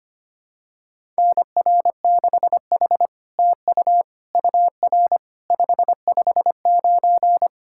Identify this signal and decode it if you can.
Morse code, heard as a keyed tone, about 25 wpm, 700 Hz. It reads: NR6H TU UR 559
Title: SOTA QSO Practice 25wpm